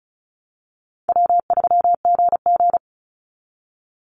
Morse code, heard as a keyed tone, about 34 wpm, 700 Hz. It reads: W3GG
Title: Easy Call Signs Repeated 3x - 35wpm